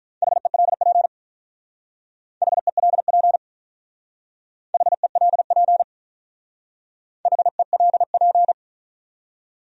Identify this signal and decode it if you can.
Morse code, heard as a keyed tone, about 44 wpm, 700 Hz. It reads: HELP HELP HELP HELP